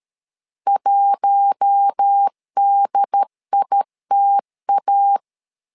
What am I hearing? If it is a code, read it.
Morse code, heard as a keyed tone, about 13 wpm, 800 Hz. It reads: 1DITA
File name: LEZIONI / 4 Q1 12wpm 2028